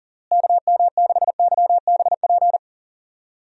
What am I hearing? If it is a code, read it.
Morse code, heard as a keyed tone, about 37 wpm, 700 Hz. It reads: KM6YBP
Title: USA Call Signs - 6 Characters - Repeated 3x - 40wpm